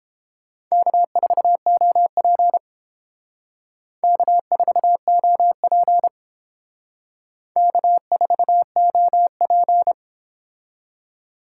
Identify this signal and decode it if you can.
Morse code, heard as a keyed tone, about 30 wpm, 700 Hz. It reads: K4OP K4OP K4OP